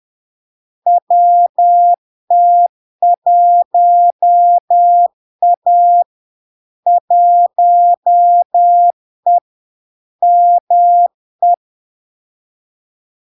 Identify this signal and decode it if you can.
Morse code, heard as a keyed tone, about 10 wpm, 700 Hz. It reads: WT1A 1E ME
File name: LARK-FD-2022-Log-in-CW-10wpm-700-Hz